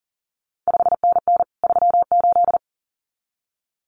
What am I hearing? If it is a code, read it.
Morse code, heard as a keyed tone, about 38 wpm, 700 Hz. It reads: 5NN 38
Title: CQ WW DX Contest Exchange Encore - 40wpm